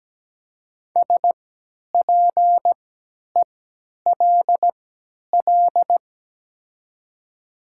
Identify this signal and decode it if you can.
Morse code, heard as a keyed tone, about 17 wpm, spaced as for 10 wpm, 700 Hz. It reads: SPELL
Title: 3x Character Spacing - Top 500 Words - 17wpm